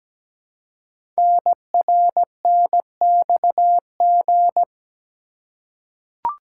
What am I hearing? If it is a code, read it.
Morse code, heard as a keyed tone, about 17 wpm, 700 Hz. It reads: NRNXG
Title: Code Groups - 5 Letters - 17wpm